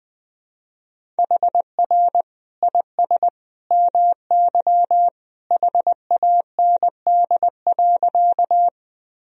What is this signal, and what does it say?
Morse code, heard as a keyed tone, about 20 wpm, 700 Hz. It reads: HR IS MY HAND.
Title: Sentences from Top 200 Words and Abbreviations 20wpm